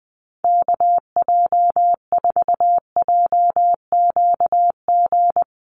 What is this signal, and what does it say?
Morse code, heard as a keyed tone, about 20 wpm, 700 Hz. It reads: KJ4JQG
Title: USA Call Signs Repeated 3x 20wpm